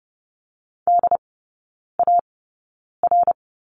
Morse code, heard as a keyed tone, about 30 wpm, spaced as for 9 wpm, 700 Hz. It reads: DAR
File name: course / 30:10wpm / 087 Lesson 15e - Number 5 - Three Characters 30wpm